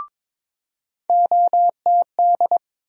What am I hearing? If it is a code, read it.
Morse code, heard as a keyed tone, about 22 wpm, 700 Hz. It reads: OTD
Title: USA Call Sign Postfixes - 3 Letters 22wpm